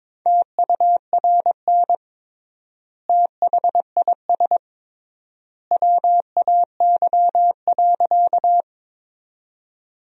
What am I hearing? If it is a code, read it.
Morse code, heard as a keyed tone, about 22 wpm, 700 Hz. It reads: TURN THIS WAY.
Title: 3x Word-Spacing - Sentences from Top 200 Words - 22wpm